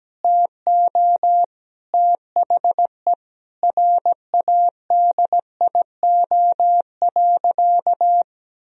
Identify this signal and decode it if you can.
Morse code, heard as a keyed tone, about 17 wpm, 700 Hz. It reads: TO THE RADIO.
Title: Sentences from Top 1000 Words - Review Part 3 - 17wpm